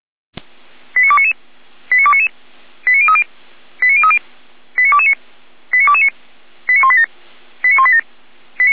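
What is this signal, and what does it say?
Five-tone selective call (ZVEI), heard as quick runs of five tones, tones of about 70 ms. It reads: 89299 89299 89030 89030 89229 89229 89187 89187
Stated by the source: ZVEI five-tone selective call 89250 received on SDR.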